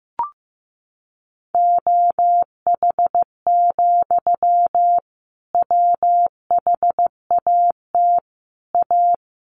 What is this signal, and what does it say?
Morse code, heard as a keyed tone, about 15 wpm, 700 Hz. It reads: OH, WHAT A
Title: Sentences from Top 1500 Words 15wpm